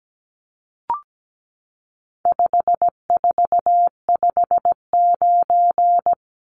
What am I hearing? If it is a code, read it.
Morse code, heard as a keyed tone, about 17 wpm, 700 Hz. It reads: 5459
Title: Numbers - 4-Digit 17wpm